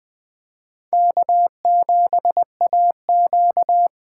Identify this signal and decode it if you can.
Morse code, heard as a keyed tone, about 20 wpm, 700 Hz. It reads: K7AQ
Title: USA Call Signs - 8x Character Spacing - 20wpm